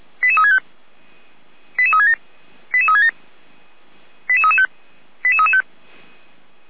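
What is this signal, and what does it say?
Five-tone selective call (ZVEI), heard as quick runs of five tones, tones of about 70 ms. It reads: 88365 88367 88367 88335 88335
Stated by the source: ZVEI five-tone selective call received on SDR.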